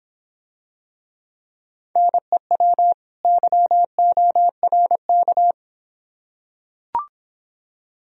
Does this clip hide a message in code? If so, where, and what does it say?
Yes: Morse code from 2.0 to 5.5 s, heard as a keyed tone, about 26 wpm, 700 Hz. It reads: NEW YORK